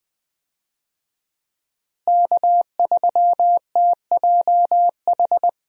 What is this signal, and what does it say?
Morse code, heard as a keyed tone, about 20 wpm, 700 Hz. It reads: K3TJH
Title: USA Call Signs - 5 Characters - Repeated 3x - 20wpm